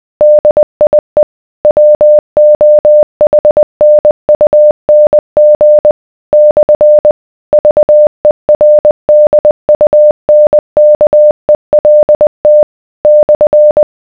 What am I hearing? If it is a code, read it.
Morse code, heard as a keyed tone, about 20 wpm, 600 Hz. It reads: DIE WOHNUNG / VERDUNKELT /